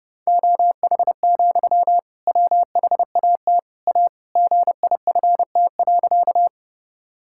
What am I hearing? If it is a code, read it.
Morse code, heard as a keyed tone, about 32 wpm, 700 Hz. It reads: OH, WHAT A GIFT.
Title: Sentences from Top 1000 Words - Review Part 6 - 30wpm